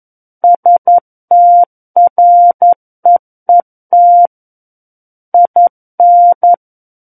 Morse code, heard as a keyed tone, about 11 wpm, 700 Hz. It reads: STREET IN
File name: 11 WPM